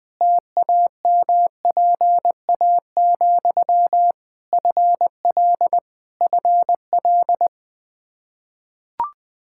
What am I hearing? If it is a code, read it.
Morse code, heard as a keyed tone, about 20 wpm, 700 Hz. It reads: TAMPA, FL FL
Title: K1USN SST QSO Practice 20wpm